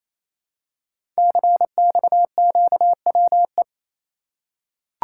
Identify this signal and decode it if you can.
Morse code, heard as a keyed tone, about 27 wpm, 700 Hz. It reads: CXQWE
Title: Code Groups - 5 Letters - 28wpm